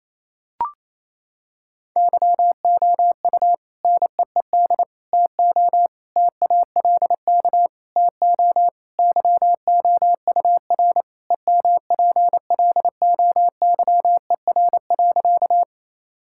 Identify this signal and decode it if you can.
Morse code, heard as a keyed tone, about 28 wpm, 700 Hz. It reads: YOU NEED TO TALK TO YOUR EMPLOYER.
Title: Sentences from Top 1600 Words 28wpm